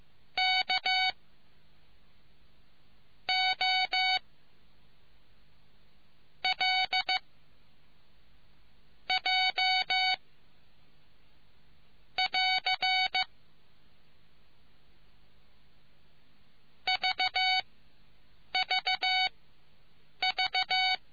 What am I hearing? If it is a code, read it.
Morse code, heard as a keyed tone, about 15 wpm, spaced as for 7 wpm, 2250 Hz. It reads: K O L J <AR> VVV